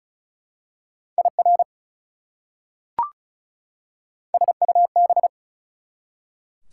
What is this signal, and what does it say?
Morse code, heard as a keyed tone, about 36 wpm, 690 Hz. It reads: IR SUB
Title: Prefixes and Suffixes 35wpm